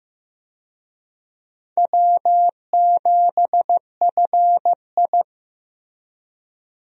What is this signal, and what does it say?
Morse code, heard as a keyed tone, about 15 wpm, 700 Hz. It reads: W7FI
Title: USA Call Signs - 4 Characters - Repeated 3x - 15wpm